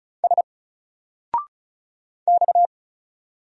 Morse code, heard as a keyed tone, about 34 wpm, 700 Hz. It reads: S X
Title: Single Letter-Number 35wpm